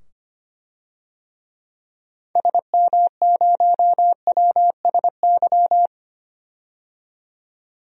Morse code, heard as a keyed tone, about 25 wpm, 700 Hz. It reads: SM0WSY
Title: DXCC-100 Call Signs - 6 Characters - 25wpm